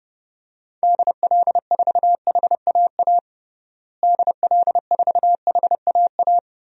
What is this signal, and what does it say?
Morse code, heard as a keyed tone, about 30 wpm, 700 Hz. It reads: DL4HAA DL4HAA
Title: DXCC-100 Call Signs - 6 Characters - Repeated 3x - 30wpm